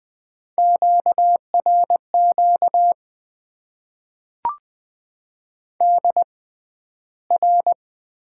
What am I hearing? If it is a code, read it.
Morse code, heard as a keyed tone, about 20 wpm, spaced as for 12 wpm, 700 Hz. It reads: QRQ DR
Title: Top 90 QSO Elements - 6x Character Spacing - 20wpm